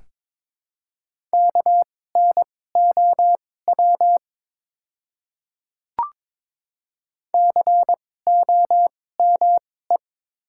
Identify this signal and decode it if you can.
Morse code, heard as a keyed tone, about 21 wpm, spaced as for 16 wpm, 700 Hz. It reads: KNOW COME
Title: 2x Character Spacing - Top 100 Words - 22wpm